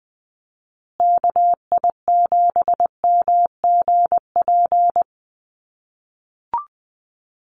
Morse code, heard as a keyed tone, about 20 wpm, 700 Hz. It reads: KI7MGP
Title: USA Call Signs Repeated 2x 20wpm